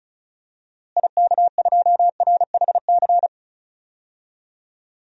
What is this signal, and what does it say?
Morse code, heard as a keyed tone, about 34 wpm, 700 Hz. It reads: IK2RHC
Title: DXCC-100 Call Signs - 6 Characters - Repeated 2x - 35wpm